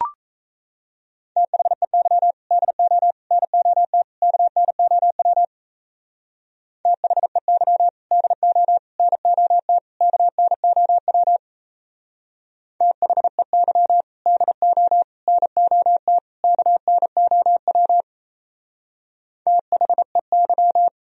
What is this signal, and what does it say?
Morse code, heard as a keyed tone, about 33 wpm, 700 Hz. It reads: THEY DO NOT KNOW THEY DO NOT KNOW THEY DO NOT KNOW THEY
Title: Speed-Racing - Sets of 4 Words 28wpm